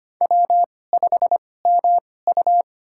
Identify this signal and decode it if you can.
Morse code, heard as a keyed tone, about 25 wpm, spaced as for 18 wpm, 700 Hz. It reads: W5MU
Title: USA Call Signs - 4 Characters - 2x Character Spacing - 25wpm